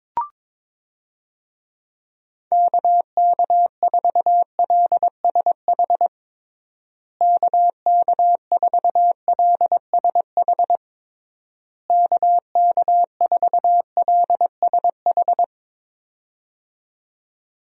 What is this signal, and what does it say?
Morse code, heard as a keyed tone, about 22 wpm, 700 Hz. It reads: KK4LSH KK4LSH KK4LSH
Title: USA Call Signs - 6 Characters - Repeated 3x - 22wpm